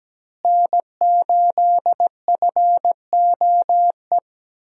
Morse code, heard as a keyed tone, about 17 wpm, 700 Hz. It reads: N8FOE
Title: USA Call Signs - 5 Characters - Repeated 2x - 17wpm